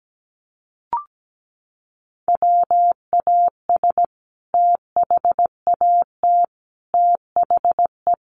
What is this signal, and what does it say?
Morse code, heard as a keyed tone, about 17 wpm, 700 Hz. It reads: WAS THAT THE
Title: Sentences from Top 1400 Words 17wpm